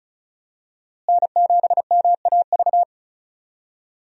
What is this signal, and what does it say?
Morse code, heard as a keyed tone, about 36 wpm, 700 Hz. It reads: N7MAV
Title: USA Call Signs - 5 Characters - Repeated 3x - 35wpm